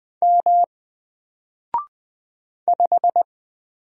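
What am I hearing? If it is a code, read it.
Morse code, heard as a keyed tone, about 20 wpm, 700 Hz. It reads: M 5
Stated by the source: Single letter and number Morse drill at 20 wpm, from N4LCD.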